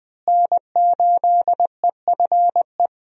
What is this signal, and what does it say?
Morse code, heard as a keyed tone, about 20 wpm, 700 Hz. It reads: N8EFE
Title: USA Call Signs - 5 Characters - Repeated 3x - 20wpm